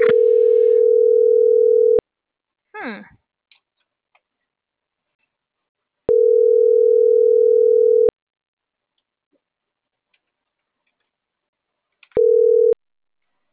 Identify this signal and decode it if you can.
Ringback tone (North America), 440 + 480 Hz.